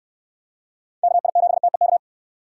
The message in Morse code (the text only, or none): KI7TEK